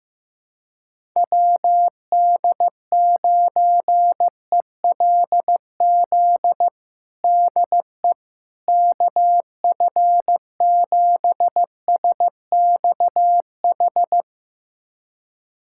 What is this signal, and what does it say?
Morse code, heard as a keyed tone, about 15 wpm, 700 Hz. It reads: WD9ELZ DE KF7SXH
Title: USA Callsign DE Callsign - 6 Characters - 15wpm